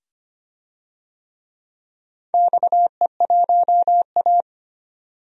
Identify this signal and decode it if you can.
Morse code, heard as a keyed tone, about 25 wpm, 700 Hz. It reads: XE1A
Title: DXCC-100 Call Signs - 4 Characters - 25wpm